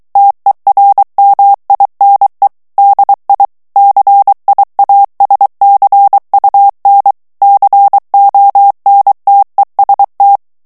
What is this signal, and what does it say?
Morse code, heard as a keyed tone, about 23 wpm, 800 Hz. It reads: TERMINE DI CIASCUN CONTEST
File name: ITALIEN texte clair 2 vitesse 22 mots minute